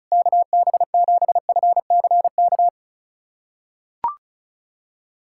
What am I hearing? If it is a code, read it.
Morse code, heard as a keyed tone, about 34 wpm, 700 Hz. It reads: KB7FCK
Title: USA Callsign DE Callsign - 6 Characters - 35wpm